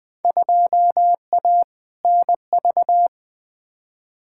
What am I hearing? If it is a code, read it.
Morse code, heard as a keyed tone, about 20 wpm, 700 Hz. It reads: 2A NV
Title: Field Day Exchanges - 20wpm